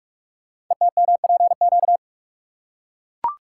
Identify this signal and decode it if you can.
Morse code, heard as a keyed tone, about 44 wpm, 700 Hz. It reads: ETMPQ
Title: Code Groups - 5 Letters - 45wpm